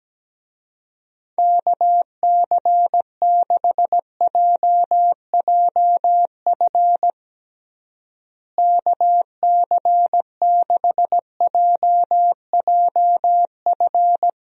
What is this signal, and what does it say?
Morse code, heard as a keyed tone, about 17 wpm, 700 Hz. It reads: KC6JJF KC6JJF
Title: USA Call Signs - 6 Characters - Repeated 2x - 17wpm